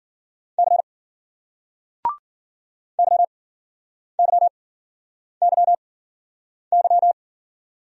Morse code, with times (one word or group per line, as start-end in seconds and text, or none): 0.6-0.8 C
3.0-7.1 YYYY